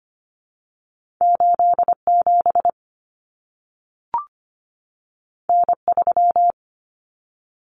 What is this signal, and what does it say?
Morse code, heard as a keyed tone, about 25 wpm, 700 Hz. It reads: 8 7 N 3
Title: Common Cut Numbers - 2-Digit 25wpm